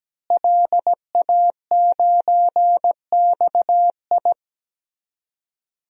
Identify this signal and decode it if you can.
Morse code, heard as a keyed tone, about 17 wpm, 700 Hz. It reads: LA9XI
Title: DXCC-100 Call Signs - 5 Characters - Repeated 3x - 17wpm